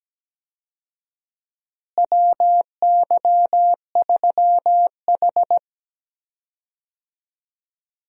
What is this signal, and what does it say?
Morse code, heard as a keyed tone, about 17 wpm, 700 Hz. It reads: WY3H
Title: USA Call Signs - 4 Characters - Repeated 2x - 17wpm